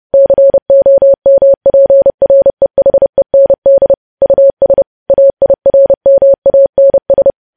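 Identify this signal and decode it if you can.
Morse code, heard as a keyed tone, about 30 wpm, 550 Hz. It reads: COMPREHEND US AIRMANS
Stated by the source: Morse code practice of news headlines at 30.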